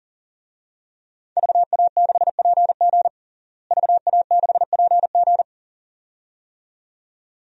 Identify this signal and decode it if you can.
Morse code, heard as a keyed tone, about 37 wpm, 700 Hz. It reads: VA6PG VA6PG
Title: DXCC-100 Call Signs - 5 Characters - Repeated 2x - 40wpm